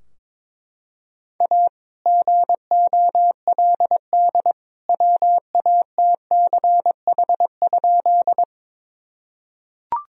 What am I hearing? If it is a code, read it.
Morse code, heard as a keyed tone, about 22 wpm, 700 Hz. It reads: A GOLD WATCH?
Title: Sentences from Top 1000 Words - Review Part 3 - 22wpm